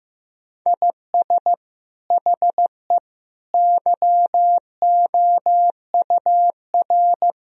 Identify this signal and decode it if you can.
Morse code, heard as a keyed tone, about 16 wpm, 700 Hz. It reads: IS HE YOUR
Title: Sentences from Top 1000 Words - Review Part 4 - 15wpm